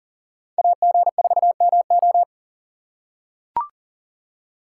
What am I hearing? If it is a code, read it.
Morse code, heard as a keyed tone, about 40 wpm, 700 Hz. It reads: AG4MO
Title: USA Call Signs - 5 Characters - 2x Character Spacing - 40wpm